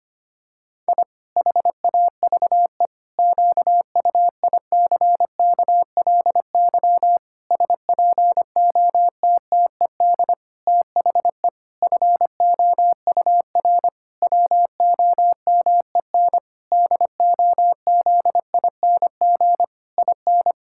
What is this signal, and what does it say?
Morse code, heard as a keyed tone, about 25 wpm, 700 Hz. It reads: I HAVE QUICKLY SPOTTED THE FOUR WOMEN DOZING IN
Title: Pangram Challenge v2 - 25wpm